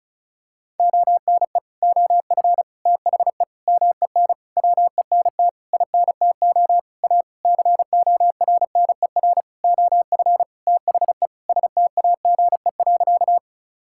Morse code, heard as a keyed tone, about 36 wpm, 700 Hz. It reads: ONE OF THE MEN WENT INTO A CORNER OF THE STAGE.
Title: Sentences from Top 1600 Words 35wpm